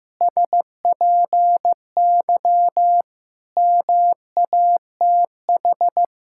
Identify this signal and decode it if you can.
Morse code, heard as a keyed tone, about 15 wpm, 700 Hz. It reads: SPY MATH
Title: Pangram Challenge v2 - 15wpm